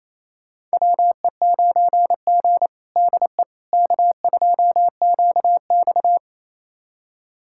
Morse code, heard as a keyed tone, about 27 wpm, 700 Hz. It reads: WE9G DE K2QX
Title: USA Callsign DE Callsign - 4 Characters - 28wpm